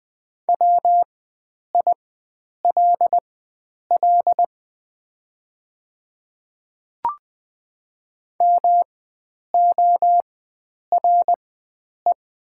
Morse code, heard as a keyed tone, about 20 wpm, spaced as for 9 wpm, 700 Hz. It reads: WILL MORE
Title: 4x Character Spacing - Top 100 Words - 20wpm